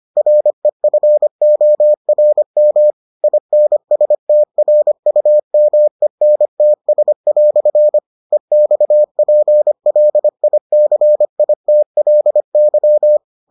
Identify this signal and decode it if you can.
Morse code, heard as a keyed tone, about 25 wpm, 600 Hz. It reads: REFORM INSTRUMENTS" EXPLICITLY